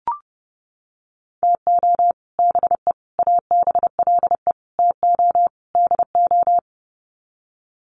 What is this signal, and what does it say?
Morse code, heard as a keyed tone, about 29 wpm, 700 Hz. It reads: TO BE ABLE TO DO
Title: Sets of 5 Words 30wpm